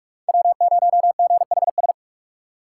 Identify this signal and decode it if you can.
Morse code, heard as a keyed tone, about 43 wpm, 700 Hz. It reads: W0GHS